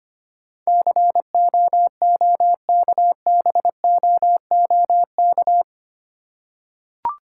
Morse code, heard as a keyed tone, about 25 wpm, 700 Hz. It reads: COOKBOOK
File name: practice / 25wpm / Difficult Words with Hints - 25wpm